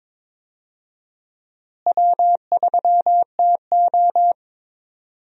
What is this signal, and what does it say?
Morse code, heard as a keyed tone, about 22 wpm, 700 Hz. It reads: W3TO
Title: USA Call Signs - 4 Characters - Repeated 3x - 22wpm